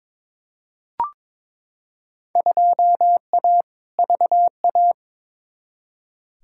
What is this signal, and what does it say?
Morse code, heard as a keyed tone, about 22 wpm, 700 Hz. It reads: 2A VA
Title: Field Day Exchanges - 22wpm